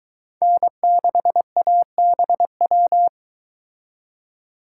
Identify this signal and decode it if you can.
Morse code, heard as a keyed tone, about 22 wpm, 700 Hz. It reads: N6ABW